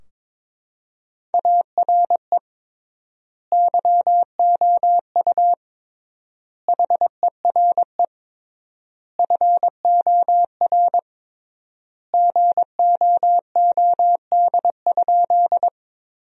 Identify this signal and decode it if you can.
Morse code, heard as a keyed tone, about 22 wpm, 700 Hz. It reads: ARE YOU HERE FOR GOOD?
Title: 3x Word-Spacing - Sentences from Top 200 Words - 22wpm